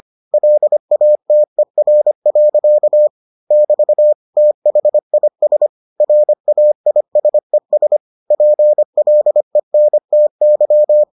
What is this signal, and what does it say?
Morse code, heard as a keyed tone, about 25 wpm, 600 Hz. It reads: LATER. <BT> THIS RAISES PLENTY